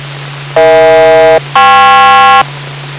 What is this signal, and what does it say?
Aircraft SELCAL, heard as two pairs of tones. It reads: EH-KP